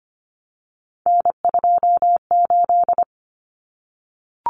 Morse code, heard as a keyed tone, about 25 wpm, 700 Hz. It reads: N28
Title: Common Cut Numbers - 3-Digit 25wpm